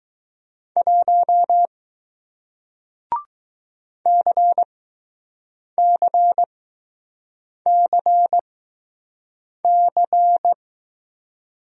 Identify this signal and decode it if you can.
Morse code, heard as a keyed tone, about 20 wpm, spaced as for 6 wpm, 700 Hz. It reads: 1 CCCC